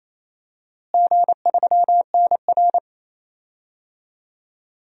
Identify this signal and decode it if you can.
Morse code, heard as a keyed tone, about 28 wpm, 700 Hz. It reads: G3NR
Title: DXCC-100 Call Signs - 4 Characters - Repeated 3x - 28wpm